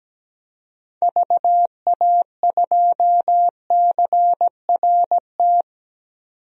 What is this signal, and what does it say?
Morse code, heard as a keyed tone, about 17 wpm, 700 Hz. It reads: VA2CRT
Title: DXCC-100 Call Signs - 6 Characters - 17wpm